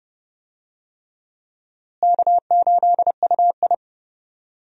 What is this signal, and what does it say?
Morse code, heard as a keyed tone, about 29 wpm, 700 Hz. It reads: K8UI
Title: USA Call Signs - 4 Characters - Repeated 3x - 30wpm